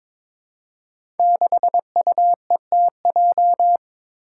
Morse code, heard as a keyed tone, about 22 wpm, 700 Hz. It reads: 6UETJ
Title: Five Challenging Letter-Number Combinations 22wpm